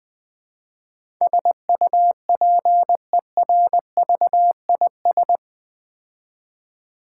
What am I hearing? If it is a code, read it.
Morse code, heard as a keyed tone, about 20 wpm, 700 Hz. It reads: SUPERVIS
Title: Build A Word - 4 Letters At A Time - 11-Letter Words - 20wpm